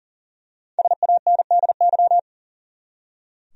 Morse code, heard as a keyed tone, about 40 wpm, 700 Hz. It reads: SANDY
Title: USA Common Names - Max 5 Letters - 40wpm